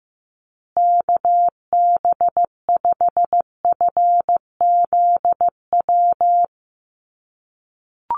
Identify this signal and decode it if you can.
Morse code, heard as a keyed tone, about 15 wpm, 700 Hz. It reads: KB5FZW
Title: USA Call Signs Repeated 3x 15wpm